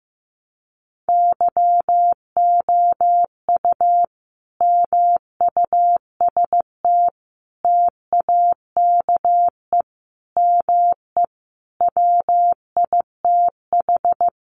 Morse code, heard as a keyed tone, about 15 wpm, 700 Hz. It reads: YOU MUST TAKE ME WITH
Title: Sentences from Top 200 Words 15wpm